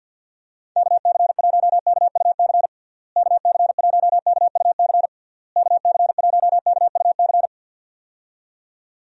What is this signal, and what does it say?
Morse code, heard as a keyed tone, about 49 wpm, 700 Hz. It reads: KC1KUC KC1KUC KC1KUC
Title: USA Call Signs - 6 Characters - Repeated 3x - 50wpm